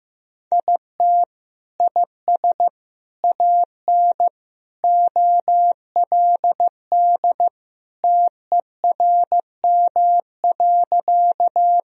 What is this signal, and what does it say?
Morse code, heard as a keyed tone, about 15 wpm, 700 Hz. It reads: IT IS AN OLD TERM.
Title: Sentences from Top 1000 Words - Review Part 4 - 15wpm